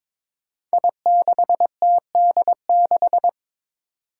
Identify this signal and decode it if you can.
Morse code, heard as a keyed tone, about 22 wpm, 700 Hz. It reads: I6TD6
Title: Five Challenging Letter-Number Combinations 22wpm